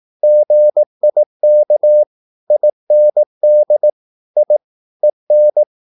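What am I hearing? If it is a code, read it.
Morse code, heard as a keyed tone, about 18 wpm, 600 Hz. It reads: GIK IND I EN
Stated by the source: Slow (QRS) Morse practice at 18 wpm, CW med Gnister.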